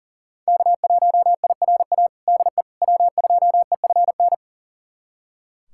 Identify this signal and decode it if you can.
Morse code, heard as a keyed tone, about 40 wpm, 700 Hz. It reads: K1IRA DE W2EFN
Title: USA Callsign DE Callsign - 5 Characters - 40wpm